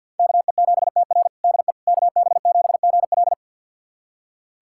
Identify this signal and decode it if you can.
Morse code, heard as a keyed tone, about 53 wpm, 700 Hz. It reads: KE7TR DE KB7GL
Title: USA Callsign DE Callsign - 5 Characters - 50wpm